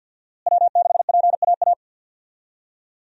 Morse code, heard as a keyed tone, about 48 wpm, 700 Hz. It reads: W6PAA
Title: USA Call Signs - 5 Characters - Repeated 2x - 50wpm